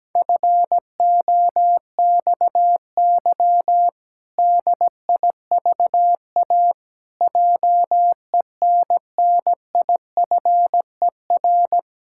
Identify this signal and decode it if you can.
Morse code, heard as a keyed tone, about 17 wpm, 700 Hz. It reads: FOXY DIVA JENNIFER
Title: Pangram Challenge v2 - 17wpm